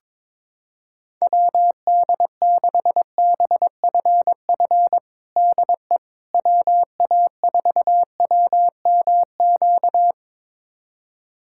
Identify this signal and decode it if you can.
Morse code, heard as a keyed tone, about 22 wpm, 700 Hz. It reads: WD6BFF DE WA4WMQ
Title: USA Callsign DE Callsign - 6 Characters - 22wpm